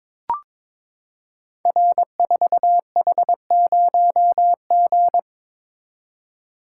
Morse code, heard as a keyed tone, about 22 wpm, 700 Hz. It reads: R4H0G
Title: Code Groups - 5 Letter:Numbers - 22wpm